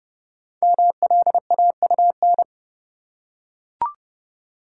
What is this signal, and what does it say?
Morse code, heard as a keyed tone, about 30 wpm, 700 Hz. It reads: MLAUN
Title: Code Groups - 5 Letters - 30wpm